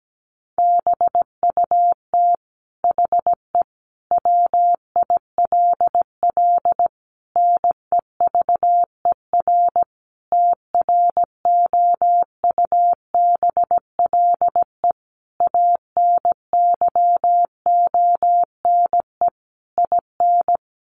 Morse code, heard as a keyed tone, about 17 wpm, 700 Hz. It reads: BUT HE WILL NEVER TROUBLE ANYONE IN
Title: Sentences from Top 1200 Words 17wpm